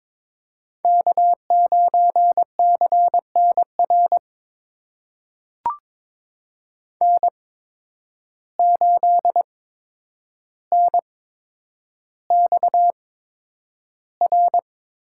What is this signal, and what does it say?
Morse code, heard as a keyed tone, about 22 wpm, spaced as for 9 wpm, 700 Hz. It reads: K9CNR N8NXR
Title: USA Call Signs - 5 Characters - 8x Character Spacing - 22wpm